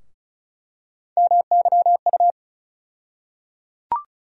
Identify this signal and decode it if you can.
Morse code, heard as a keyed tone, about 34 wpm, 700 Hz. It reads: MYU
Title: USA Call Sign Postfixes - 3 Letters 35wpm